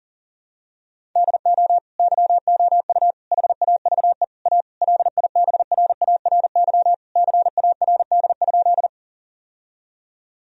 Morse code, heard as a keyed tone, about 40 wpm, 700 Hz. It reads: DO YOU HAVE A LIBRARY CARD?
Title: Sentences from Top 1000 Words - Review Part 4 - 40wpm